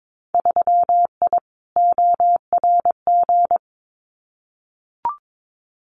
Morse code, heard as a keyed tone, about 22 wpm, 700 Hz. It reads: 3I ORG
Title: Winter Field Day Exchanges Encore 22wpm